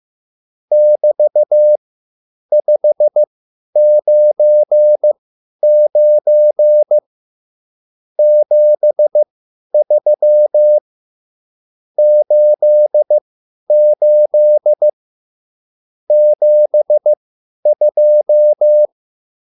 Morse code, heard as a keyed tone, about 15 wpm, spaced as for 10 wpm, 600 Hz. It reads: <BT>599 73 88 72
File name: QSL / CW lesson